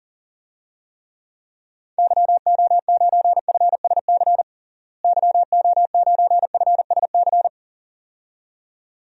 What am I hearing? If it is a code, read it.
Morse code, heard as a keyed tone, about 40 wpm, 700 Hz. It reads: YO9FSC YO9FSC
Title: DXCC-100 Call Signs - 6 Characters - Repeated 2x - 40wpm